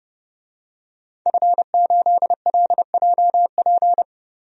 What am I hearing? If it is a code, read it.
Morse code, heard as a keyed tone, about 31 wpm, 700 Hz. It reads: F8LJP
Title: DXCC-100 Call Signs - 5 Characters - 30wpm